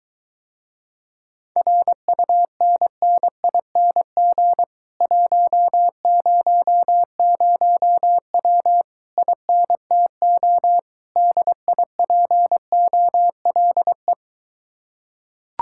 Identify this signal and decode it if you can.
Morse code, heard as a keyed tone, about 23 wpm, 700 Hz. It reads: RUNNING 100W INTO DIPOLE